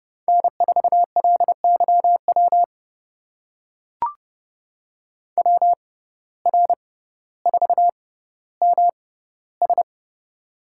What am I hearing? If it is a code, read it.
Morse code, heard as a keyed tone, about 30 wpm, spaced as for 15 wpm, 700 Hz. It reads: N4LYW WR4MS
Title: USA Call Signs - 6x Character Spacing - 30wpm